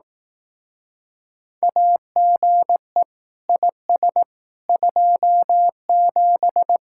Morse code, heard as a keyed tone, about 18 wpm, 700 Hz. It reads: AGE IS 27
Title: Speed-Racing Common QSO Phrases Encore - 15wpm